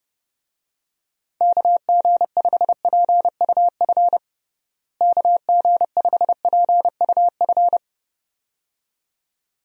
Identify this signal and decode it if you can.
Morse code, heard as a keyed tone, about 32 wpm, 700 Hz. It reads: KG5PUF KG5PUF
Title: USA Call Signs - 6 Characters - Repeated 2x - 30wpm